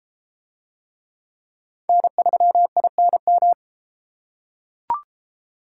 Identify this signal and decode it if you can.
Morse code, heard as a keyed tone, about 32 wpm, 700 Hz. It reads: N3INM